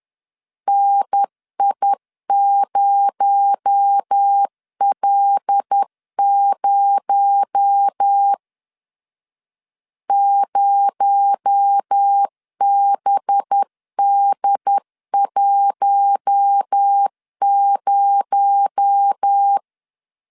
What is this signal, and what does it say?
Morse code, heard as a keyed tone, about 10 wpm, 800 Hz. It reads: NI0L0 0BD10
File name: LEZIONI / 6 PKX0 / 6 PKX0 10wpm 2707